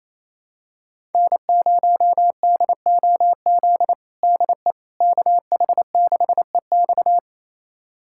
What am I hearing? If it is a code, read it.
Morse code, heard as a keyed tone, about 27 wpm, 700 Hz. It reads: N0DOZ DE KH6EX
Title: USA Callsign DE Callsign - 5 Characters - 28wpm